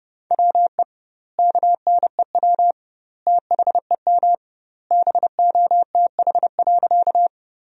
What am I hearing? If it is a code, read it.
Morse code, heard as a keyed tone, about 30 wpm, 700 Hz. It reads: WE KNEW THEM BOTH.
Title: 2x Word-Spacing - Sentences from Top 400 Words - 30wpm